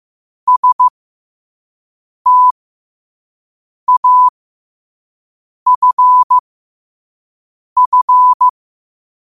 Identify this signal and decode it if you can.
Morse code, heard as a keyed tone, about 15 wpm, spaced as for 5 wpm, 1000 Hz. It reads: STAFF
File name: qotdfiles / 5wpm